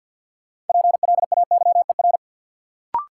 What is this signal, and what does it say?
Morse code, heard as a keyed tone, about 53 wpm, 700 Hz. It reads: PLAYER
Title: Sentences from Top 1000 Words - Review Part 4 - New Word Drill - 50wpm